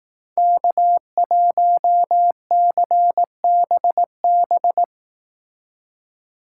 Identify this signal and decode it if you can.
Morse code, heard as a keyed tone, about 18 wpm, 700 Hz. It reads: K1CBB
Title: Speed Racing - USA 1x3 Call Signs 15wpm